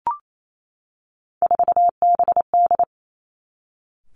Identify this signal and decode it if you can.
Morse code, heard as a keyed tone, about 28 wpm, 700 Hz. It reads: 4BD
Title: Three Challenging Letter-Number Combinations 28wpm